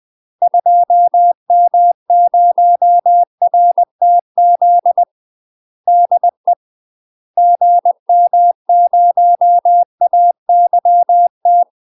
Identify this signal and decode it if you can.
Morse code, heard as a keyed tone, about 20 wpm, 700 Hz. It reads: 2M0RTZ DE GM0AYT
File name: UK MASTER QSOs TRAINING / 20WPM / UK MASTER QSOs 0040